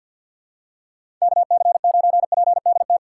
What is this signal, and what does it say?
Morse code, heard as a keyed tone, about 49 wpm, 700 Hz. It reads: KC9PDT
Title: USA Call Signs - 6 Characters - 4x Character Spacing - 50wpm